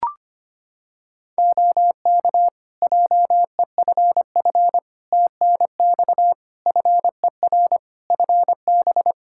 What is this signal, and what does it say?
Morse code, heard as a keyed tone, about 25 wpm, 700 Hz. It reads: OK JEFF TNX FER FB
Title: Practice QSO - Exchange 4 - Ending - 25wpm